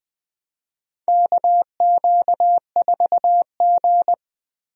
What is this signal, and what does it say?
Morse code, heard as a keyed tone, about 20 wpm, 700 Hz. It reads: KQ4G
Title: USA Call Signs - 4 Characters - Repeated 2x - 20wpm